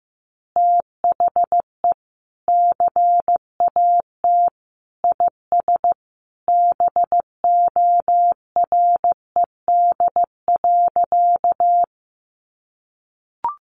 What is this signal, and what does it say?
Morse code, heard as a keyed tone, about 15 wpm, 700 Hz. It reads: THE CAT IS BORED.
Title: Easy Phrases 15wpm